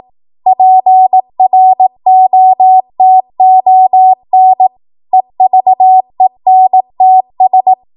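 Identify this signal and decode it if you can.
Morse code, heard as a keyed tone, about 18 wpm, 750 Hz. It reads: PROTON EVENTS
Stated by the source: ARRL 18 wpm Morse code practice archive transmission.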